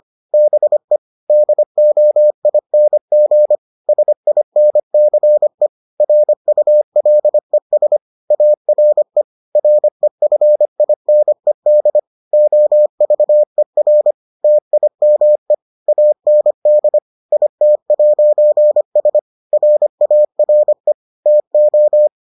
Morse code, heard as a keyed tone, about 25 wpm, 600 Hz. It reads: BE DOING SINCE RULES ARE REFINED OVER TIME AND IT'S RARE TO